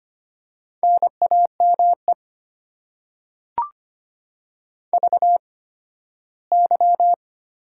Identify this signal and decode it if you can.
Morse code, heard as a keyed tone, about 25 wpm, spaced as for 15 wpm, 690 Hz. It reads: NAME VY
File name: Top 90 QSO Elements - 8x Character Spacing - 25wpm